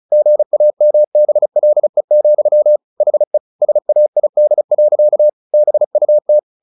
Morse code, heard as a keyed tone, about 34 wpm, 600 Hz. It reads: GAMBLE, HE SAID. BUT
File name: the old man and the sea / the old man and the sea 1 35wpm